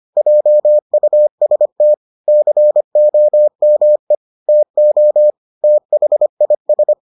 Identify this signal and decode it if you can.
Morse code, heard as a keyed tone, about 25 wpm, 600 Hz. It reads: JUST COME TO THIS